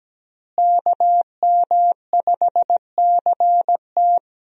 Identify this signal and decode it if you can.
Morse code, heard as a keyed tone, about 17 wpm, 700 Hz. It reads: KM5CT
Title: USA Call Signs - 5 Characters - Repeated 2x - 17wpm